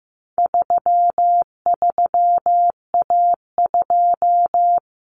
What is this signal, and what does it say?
Morse code, heard as a keyed tone, about 15 wpm, 700 Hz. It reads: 33A2
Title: Common Cut Numbers - 4-Digit 15wpm